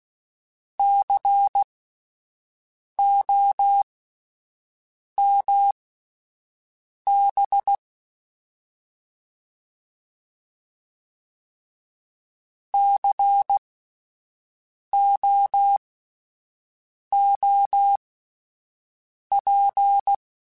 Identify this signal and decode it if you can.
Morse code, heard as a keyed tone, about 16 wpm, spaced as for 5 wpm, 800 Hz. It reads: COMB COOP
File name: Spaced Alphas / Spaced Alphas mp3s / Cs sp @10